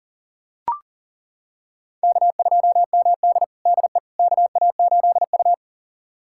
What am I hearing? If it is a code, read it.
Morse code, heard as a keyed tone, about 40 wpm, 700 Hz. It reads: K2MD DE KA8U
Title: USA Callsign DE Callsign - 4 Characters - 40wpm